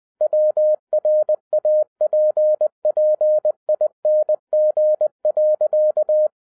Morse code, heard as a keyed tone, about 20 wpm, 610 Hz. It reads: WRAPPING.
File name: 2011-07-mp3 / n43 Czech Army Morse Keys 20wpm